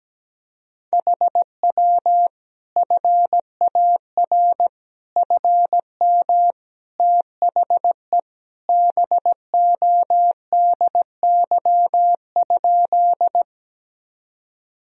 Morse code, heard as a keyed tone, about 17 wpm, 700 Hz. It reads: HW FAR FM THE BODY?
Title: Sentences from Top 400 Words and Abbreviations 17wpm